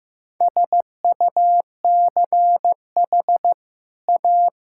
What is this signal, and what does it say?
Morse code, heard as a keyed tone, about 15 wpm, 700 Hz. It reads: SUCH A
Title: Sentences from Top 1600 Words 15wpm